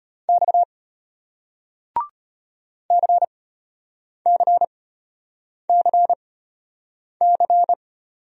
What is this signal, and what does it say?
Morse code, heard as a keyed tone, about 34 wpm, spaced as for 8 wpm, 700 Hz. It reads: X CCCC